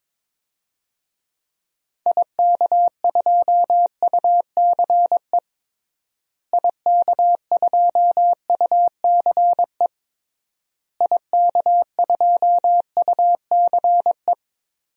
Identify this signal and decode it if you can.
Morse code, heard as a keyed tone, about 22 wpm, 700 Hz. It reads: IK2UCE IK2UCE IK2UCE
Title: DXCC-100 Call Signs - 6 Characters - Repeated 3x - 22wpm